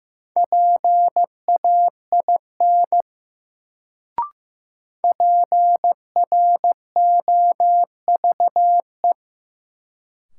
Morse code, heard as a keyed tone, about 15 wpm, 700 Hz. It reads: PAIN PROVE
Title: Sentences from Top 1600 Words - New Word Drill - 15wpm